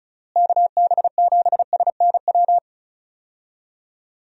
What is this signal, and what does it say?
Morse code, heard as a keyed tone, about 34 wpm, 700 Hz. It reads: KB7SNW
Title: USA Call Signs - 6 Characters - Repeated 3x - 35wpm